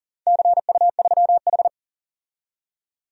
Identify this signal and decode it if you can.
Morse code, heard as a keyed tone, about 40 wpm, 700 Hz. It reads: CU3H
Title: DXCC-100 Call Signs - 4 Characters - 40wpm